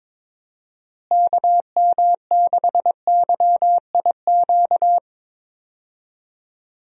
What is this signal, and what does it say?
Morse code, heard as a keyed tone, about 22 wpm, 700 Hz. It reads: KM6YIQ
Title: USA Call Signs - 6 Characters - Repeated 2x - 22wpm